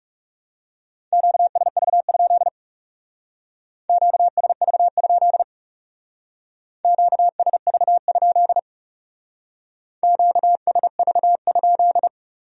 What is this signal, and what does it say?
Morse code, heard as a keyed tone, about 40 wpm, 700 Hz. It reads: QSV? QSV? QSV? QSV?